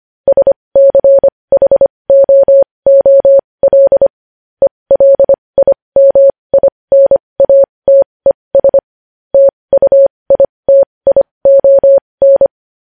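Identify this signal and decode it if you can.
Morse code, heard as a keyed tone, about 25 wpm, spaced as for 20 wpm, 550 Hz. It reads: SCHOOL ELIMINATES TUITION
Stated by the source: Morse code at 20 wpm.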